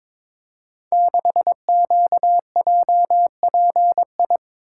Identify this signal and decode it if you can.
Morse code, heard as a keyed tone, about 22 wpm, 700 Hz. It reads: 6QJPI
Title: Code Groups - 5 Letter:Numbers - 22wpm